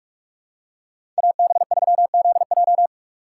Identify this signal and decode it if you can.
Morse code, heard as a keyed tone, about 45 wpm, 700 Hz. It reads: AB3ZJ
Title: USA Call Signs - 5 Characters - 4x Character Spacing - 45wpm